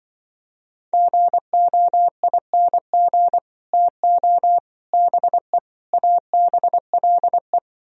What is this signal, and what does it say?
Morse code, heard as a keyed tone, about 25 wpm, 700 Hz. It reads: GOING TO BE ABLE
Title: Speed-Racing - Sets of 4 Words 20wpm